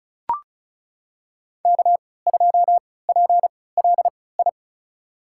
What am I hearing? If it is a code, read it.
Morse code, heard as a keyed tone, about 34 wpm, spaced as for 20 wpm, 700 Hz. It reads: K2PLI
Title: USA Call Signs - 3x Character Spacing - 35wpm